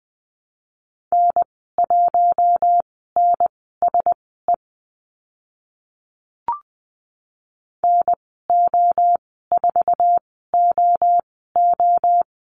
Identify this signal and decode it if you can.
Morse code, heard as a keyed tone, about 20 wpm, spaced as for 14 wpm, 700 Hz. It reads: N1NSE NO4OO
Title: 062 Lesson 11g - Letter h - Call Signs 20wpm